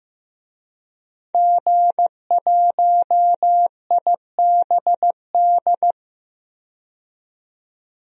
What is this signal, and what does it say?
Morse code, heard as a keyed tone, about 15 wpm, 700 Hz. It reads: G1IBD
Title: DXCC-100 Call Signs - 5 Characters - 15wpm